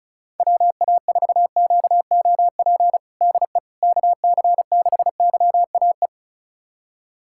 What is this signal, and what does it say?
Morse code, heard as a keyed tone, about 36 wpm, 700 Hz. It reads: WA4QOP DE KC6YAE
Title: USA Callsign DE Callsign 35wpm